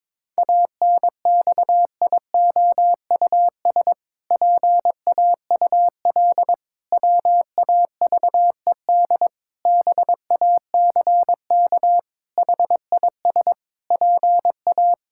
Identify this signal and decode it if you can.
Morse code, heard as a keyed tone, about 22 wpm, 700 Hz. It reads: ANXIOUS PAUL WAVED BACK HIS PA
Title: Pangram Challenge v2 - 22wpm